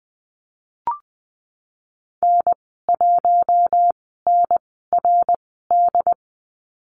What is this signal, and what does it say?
Morse code, heard as a keyed tone, about 19 wpm, spaced as for 14 wpm, 700 Hz. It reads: N1NRD
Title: 200 Lesson 32g - Letter k - Call Signs 20wpm